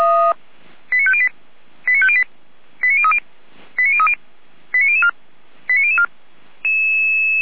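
Five-tone selective call (ZVEI), heard as quick runs of five tones, tones of about 70 ms. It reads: 89508 89508 89030 89030 89004 89004
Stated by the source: ZVEI five-tone selective call received on an SDR, from the Feuerwehr Orlamünde audio server.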